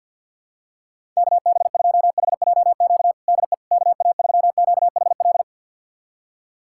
Morse code, heard as a keyed tone, about 48 wpm, 700 Hz. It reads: KB2HJQ DE KA3QHL
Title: USA Callsign DE Callsign 50wpm